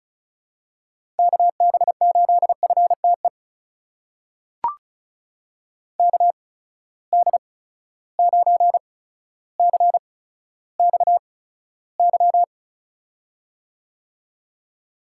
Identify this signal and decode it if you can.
Morse code, heard as a keyed tone, about 36 wpm, spaced as for 15 wpm, 700 Hz. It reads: KB8FTE KD9CXY
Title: USA Call Signs - 6 Characters - 8x Character Spacing - 35wpm